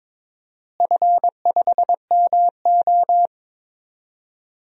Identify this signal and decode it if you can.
Morse code, heard as a keyed tone, about 22 wpm, 700 Hz. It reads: F5MO
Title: DXCC-100 Call Signs - 4 Characters - Repeated 2x - 22wpm